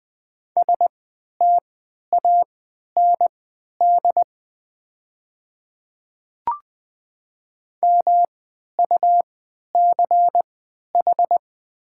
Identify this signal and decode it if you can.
Morse code, heard as a keyed tone, about 20 wpm, spaced as for 11 wpm, 700 Hz. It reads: STAND MUCH
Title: 3x Character Spacing - Top 500 Words - 20wpm